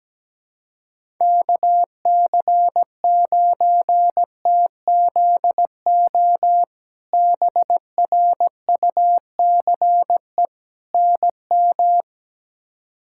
Morse code, heard as a keyed tone, about 17 wpm, 700 Hz. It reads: KC9TZO BRUCE NM
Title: K1USN SST QSO Practice - 17wpm